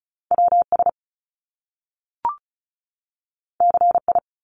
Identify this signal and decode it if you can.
Morse code, heard as a keyed tone, about 36 wpm, 700 Hz. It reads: WS CI
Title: Top Two Letter Combinations in Words 35wpm